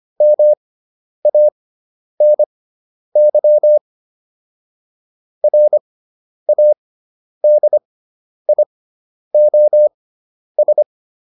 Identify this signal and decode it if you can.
Morse code, heard as a keyed tone, about 25 wpm, spaced as for 10 wpm, 600 Hz. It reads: MANY RADIOS